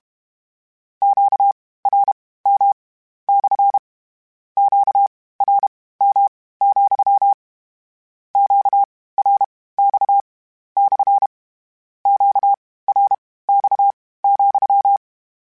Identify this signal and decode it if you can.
Morse code, heard as a keyed tone, about 32 wpm, spaced as for 18 wpm, 800 Hz. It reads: QRM / QRM, QRX / QRX,
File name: Q Calls / Q Calls-mp3s / Q CALLS SPSL @30